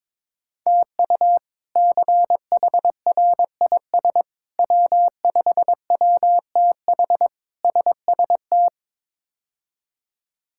Morse code, heard as a keyed tone, about 22 wpm, 700 Hz. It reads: TU CHRIS W5WTH SST
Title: K1USN SST QSO Practice - 22wpm